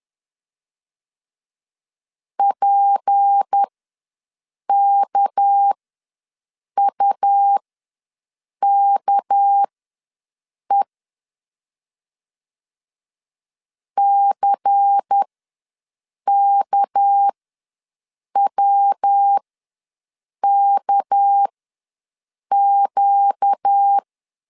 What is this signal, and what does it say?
Morse code, heard as a keyed tone, about 10 wpm, spaced as for 6 wpm, 800 Hz. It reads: PKUKE CKWKQ
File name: LEZIONI / 6 PKX0 / 6 PK ws 10wpm 3745